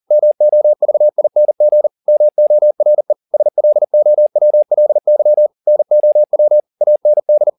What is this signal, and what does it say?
Morse code, heard as a keyed tone, about 40 wpm, 600 Hz. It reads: MOVING MORE SLOWLY NOW AND